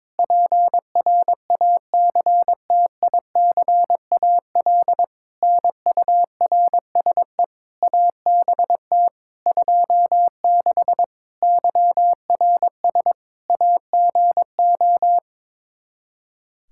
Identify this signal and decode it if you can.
Morse code, heard as a keyed tone, about 22 wpm, 700 Hz. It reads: PRACTICAL NURSE ABT 26 YRS AGO
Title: Practice QSO - Exchange 3 - More Info - 22wpm